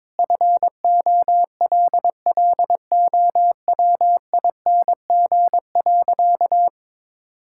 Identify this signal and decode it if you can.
Morse code, heard as a keyed tone, about 22 wpm, 700 Hz. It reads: FOLLOWING.
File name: Sentences with 9-Word Phrases - 22wpm